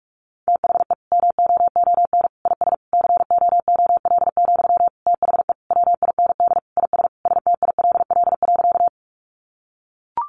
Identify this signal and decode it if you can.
Morse code, heard as a keyed tone, about 44 wpm, 700 Hz. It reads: THE MOON IS COOL, THE WIND IS STILL.
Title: Sentences from Top 1200 Words 45wpm